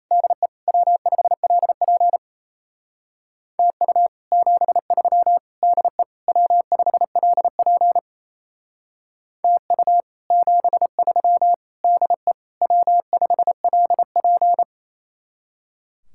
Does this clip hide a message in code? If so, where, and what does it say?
Yes: Morse code from 0.1 to 14.6 s, heard as a keyed tone, about 37 wpm, 700 Hz. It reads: DE W5LP TU 73 DE W5LP TU 73 DE W5LP